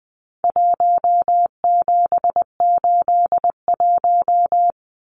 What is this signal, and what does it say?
Morse code, heard as a keyed tone, about 20 wpm, 700 Hz. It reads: 1781
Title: Numbers - Contest Style - 4-Digit 20wpm